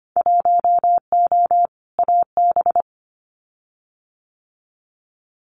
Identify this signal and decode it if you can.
Morse code, heard as a keyed tone, about 25 wpm, 690 Hz. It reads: 1O AB
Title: Winter Field Day Exchanges 25wpm